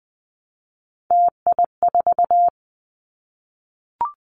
Morse code, heard as a keyed tone, about 20 wpm, 700 Hz. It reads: TI4
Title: Callsign Prefixes for Top 100 DXCC - 20wpm